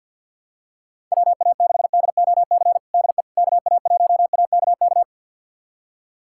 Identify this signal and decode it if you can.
Morse code, heard as a keyed tone, about 52 wpm, 690 Hz. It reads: WA6DOC DE KA1AKK
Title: USA Callsign DE Callsign - 6 Characters - 50wpm